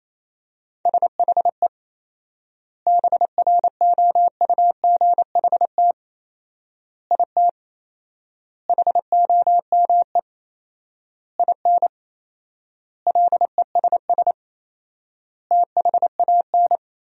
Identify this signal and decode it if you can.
Morse code, heard as a keyed tone, about 27 wpm, 700 Hz. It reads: SHE BROUGHT IT HOME IN LESS THAN
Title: 4x Word-Spacing - Sentences from Top 500 Words - 28wpm